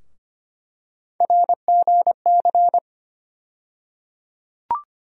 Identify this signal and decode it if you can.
Morse code, heard as a keyed tone, about 25 wpm, 700 Hz. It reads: RGC
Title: USA Call Sign Postfixes - 3 Letters 25wpm